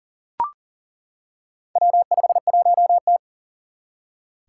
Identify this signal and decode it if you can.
Morse code, heard as a keyed tone, about 40 wpm, 700 Hz. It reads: W51T
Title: Four Challenging Letter-Number Combinations 40wpm